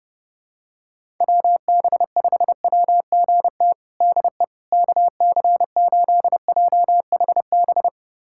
Morse code, heard as a keyed tone, about 30 wpm, 700 Hz. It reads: WB5WGT DE KC8JHB
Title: USA Callsign DE Callsign 30wpm